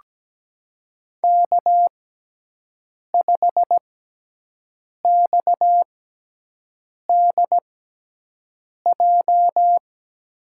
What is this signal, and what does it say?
Morse code, heard as a keyed tone, about 17 wpm, spaced as for 6 wpm, 700 Hz. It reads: K5XDJ
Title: USA Call Signs - 5 Characters - 6x Character Spacing - 17wpm